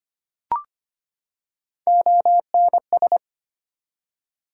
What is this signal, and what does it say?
Morse code, heard as a keyed tone, about 25 wpm, 700 Hz. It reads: ONS
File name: practice / 25wpm / ARRL Sections v2 - 25wpm